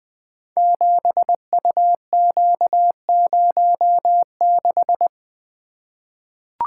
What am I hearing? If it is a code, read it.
Morse code, heard as a keyed tone, about 20 wpm, 700 Hz. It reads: 7UQ06
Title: Code Groups - 5 Letter:Numbers - 20wpm